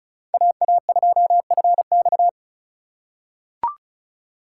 Morse code, heard as a keyed tone, about 36 wpm, 700 Hz. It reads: AA2FX
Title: USA Call Signs - 5 Characters - 2x Character Spacing - 35wpm